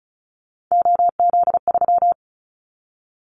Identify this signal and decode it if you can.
Morse code, heard as a keyed tone, about 34 wpm, 700 Hz. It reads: OZ3
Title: Callsign Prefixes for Top 100 DXCC - 35wpm